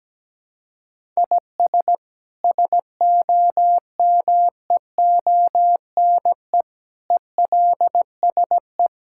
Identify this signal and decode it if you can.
Morse code, heard as a keyed tone, about 17 wpm, 700 Hz. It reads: IS SOMEONE ELSE
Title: Sentences from Top 1000 Words - Review Part 5 - 17wpm